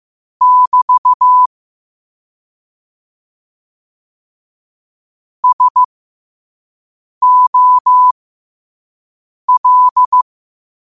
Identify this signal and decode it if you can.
Morse code, heard as a keyed tone, about 15 wpm, spaced as for 5 wpm, 1000 Hz. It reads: <BT> SOL
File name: qotdfiles / 5wpm